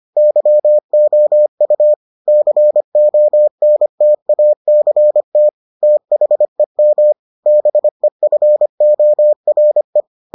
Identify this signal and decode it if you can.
Morse code, heard as a keyed tone, about 25 wpm, 600 Hz. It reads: YOU CONTACT THEM BEFORE